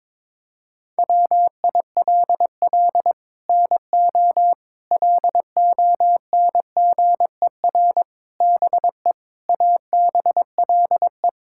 Morse code, heard as a keyed tone, about 22 wpm, 700 Hz. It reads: WILL NO LONGER BE ABLE
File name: practice / 22wpm / Sets of 6 Words - 22wpm